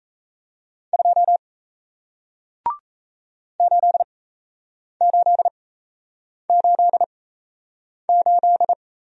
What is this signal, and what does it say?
Morse code, heard as a keyed tone, about 38 wpm, spaced as for 8 wpm, 700 Hz. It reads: 2 8888